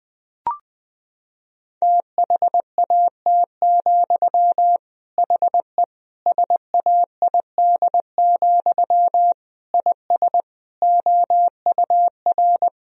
Morse code, heard as a keyed tone, about 21 wpm, 700 Hz. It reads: THAT, HE SAID, IS OUR
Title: Sentences from Top 1000 Words - Review Part 3 - 20wpm